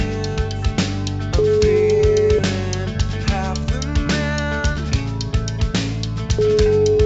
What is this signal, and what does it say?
Ringback tone (Europe, 425 Hz), with noise.